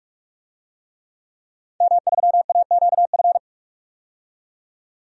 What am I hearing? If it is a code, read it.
Morse code, heard as a keyed tone, about 44 wpm, 700 Hz. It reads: M3AQF
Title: DXCC-100 Call Signs - 5 Characters - 45wpm